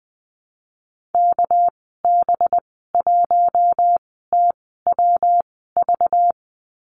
Morse code, heard as a keyed tone, about 20 wpm, spaced as for 14 wpm, 700 Hz. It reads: KB1TWV
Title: 200 Lesson 32g - Letter k - Call Signs 20wpm